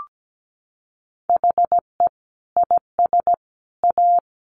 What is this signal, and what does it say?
Morse code, heard as a keyed tone, about 17 wpm, 700 Hz. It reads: HE IS A
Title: Sentences from Top 1500 Words 17wpm